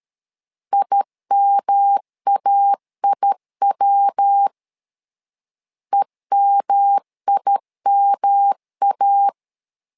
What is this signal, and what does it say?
Morse code, heard as a keyed tone, about 13 wpm, 800 Hz. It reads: IMAIW EMIMA
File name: LEZIONI / 3 MWRO / 3 MW 12wpm 1633